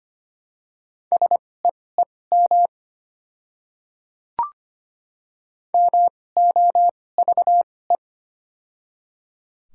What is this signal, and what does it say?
Morse code, heard as a keyed tone, about 25 wpm, spaced as for 18 wpm, 700 Hz. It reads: SEEM MOVE
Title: 2x Character Spacing - Top 300 Words - 25wpm